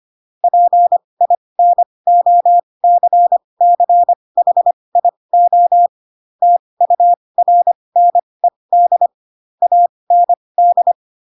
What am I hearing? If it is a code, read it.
Morse code, heard as a keyed tone, about 25 wpm, spaced as for 20 wpm, 700 Hz. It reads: PINOCCHIO TURNED AND
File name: Pinocchio / Pinocchio ch4 15 25 ik1hge